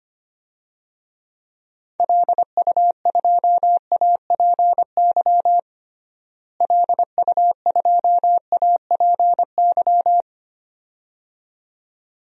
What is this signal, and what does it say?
Morse code, heard as a keyed tone, about 25 wpm, 700 Hz. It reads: LU2APY LU2APY
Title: DXCC-100 Call Signs - 6 Characters - Repeated 2x - 25wpm